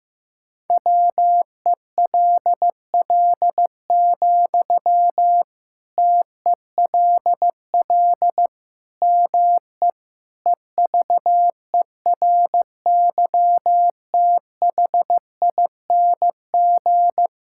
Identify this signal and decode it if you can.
Morse code, heard as a keyed tone, about 15 wpm, 700 Hz. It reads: WELL, TELL ME EVERYTHING
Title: Sentences from Top 1000 Words - Review Part 6 - 15wpm